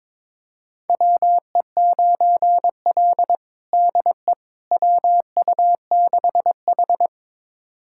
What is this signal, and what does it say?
Morse code, heard as a keyed tone, about 22 wpm, 700 Hz. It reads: WE9L DE WU6H
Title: USA Callsign DE Callsign - 4 Characters - 22wpm